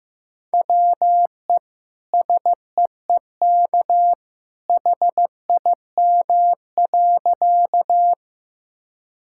Morse code, heard as a keyed tone, about 15 wpm, 700 Hz. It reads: WE SEEK HIM.
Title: Sentences from Top 1600 Words 15wpm